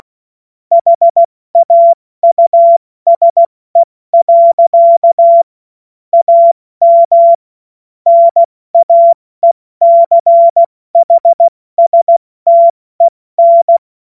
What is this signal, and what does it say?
Morse code, heard as a keyed tone, about 16 wpm, 680 Hz. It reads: HAUSE. AM NAECHSTEN
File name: Morsekurs / Witze / Witz 21. September 2021, 70 F80